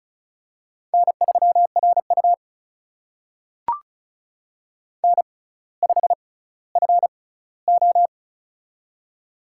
Morse code, heard as a keyed tone, about 36 wpm, spaced as for 18 wpm, 700 Hz. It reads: N3RU N5FO
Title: USA Call Signs - 4 Characters - 6x Character Spacing - 35wpm